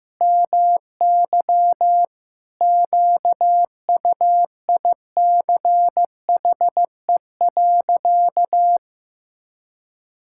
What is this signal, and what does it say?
Morse code, heard as a keyed tone, about 15 wpm, 700 Hz. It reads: MY QUICHE.
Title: Pangram Challenge v2 - 15wpm